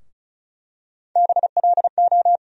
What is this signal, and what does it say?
Morse code, heard as a keyed tone, about 34 wpm, 700 Hz. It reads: BLO
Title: USA Call Sign Postfixes - 3 Letters 35wpm